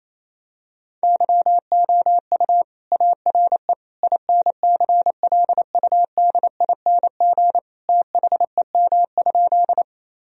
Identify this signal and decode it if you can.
Morse code, heard as a keyed tone, about 28 wpm, 700 Hz. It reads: YOU ARE INCLUDING THEM?
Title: Sentences from Top 1000 Words - Review Part 4 - 28wpm